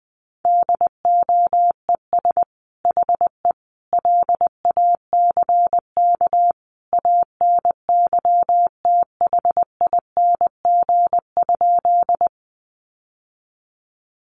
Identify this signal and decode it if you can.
Morse code, heard as a keyed tone, about 19 wpm, 700 Hz. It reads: DOES HE LACK ANYTHING?
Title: Sentences from Top 1100 Words 20wpm